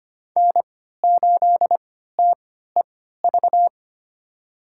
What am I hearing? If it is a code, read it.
Morse code, heard as a keyed tone, about 25 wpm, spaced as for 14 wpm, 700 Hz. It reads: N8TEV
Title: USA Call Signs - 5 Characters - 3x Character Spacing - 25wpm